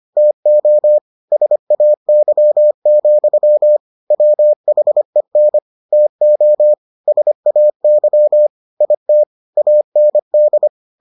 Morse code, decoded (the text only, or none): TO SAY, WHEN TO SAY IT AND